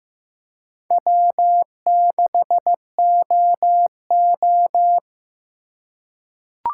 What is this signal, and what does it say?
Morse code, heard as a keyed tone, about 15 wpm, 700 Hz. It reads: W6OO
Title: USA Call Signs - 4 Characters - 2x Character Spacing - 15wpm